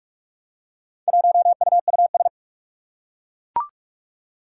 Morse code, heard as a keyed tone, about 42 wpm, 700 Hz. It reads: 1UUS
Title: Four Challenging Letter-Number Combinations 45wpm